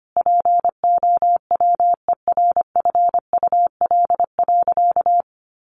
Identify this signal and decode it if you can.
Morse code, heard as a keyed tone, about 25 wpm, 700 Hz. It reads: POWERFUL.
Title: Sentences from Top 1300 Words 25wpm